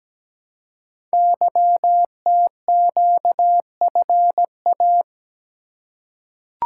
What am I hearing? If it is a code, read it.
Morse code, heard as a keyed tone, about 17 wpm, 700 Hz. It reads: YTQFA
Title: Code Groups - 5 Letters - 17wpm